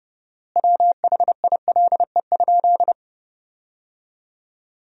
Morse code, heard as a keyed tone, about 30 wpm, 700 Hz. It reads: WHILE?
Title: 4x Word-Spacing - Sentences from Top 300 Words - 30wpm